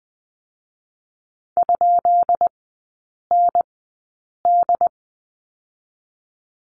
Morse code, heard as a keyed tone, about 20 wpm, 700 Hz. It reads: ? N D
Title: 133 Lesson 22e - Question Mark - Three Characters 20wpm